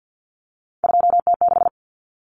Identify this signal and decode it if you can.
Morse code, heard as a keyed tone, about 50 wpm, 700 Hz. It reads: 2T6
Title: Common Cut Numbers - 3-Digit 50wpm